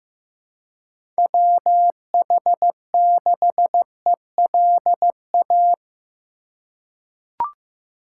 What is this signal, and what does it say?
Morse code, heard as a keyed tone, about 15 wpm, 700 Hz. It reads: WH6ELA
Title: USA Call Signs - 6 Characters - 3x Character Spacing - 15wpm